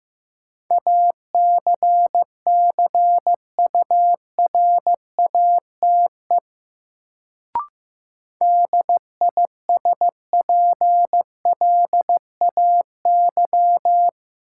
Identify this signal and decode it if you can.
Morse code, heard as a keyed tone, about 15 wpm, 700 Hz. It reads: ACCURATE DISPLAY
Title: Sentences from Top 1200 Words - New Word Drill - 15wpm